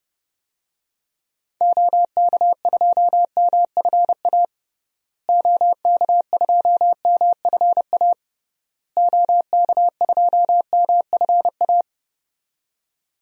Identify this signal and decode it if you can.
Morse code, heard as a keyed tone, about 30 wpm, 700 Hz. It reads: OK2MFA OK2MFA OK2MFA
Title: DXCC-100 Call Signs - 6 Characters - Repeated 3x - 30wpm